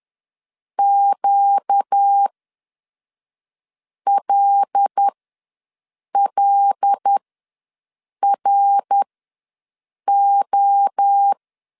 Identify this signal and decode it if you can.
Morse code, heard as a keyed tone, about 10 wpm, spaced as for 5 wpm, 800 Hz. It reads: QLLRO
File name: LEZIONI / 5 LBVC / 5 L ws 10wpm 3636